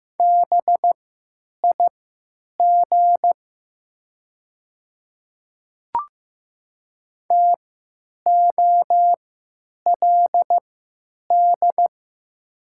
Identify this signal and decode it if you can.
Morse code, heard as a keyed tone, about 15 wpm, spaced as for 9 wpm, 700 Hz. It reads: BIG TOLD
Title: 3x Character Spacing - Top 500 Words - 15wpm